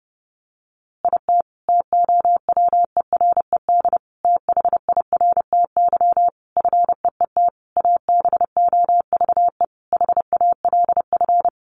Morse code, heard as a keyed tone, about 30 wpm, 700 Hz. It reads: IT TOWERED THIRTY FEET ABOVE HALF
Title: A Sound Of Thunder 30wpm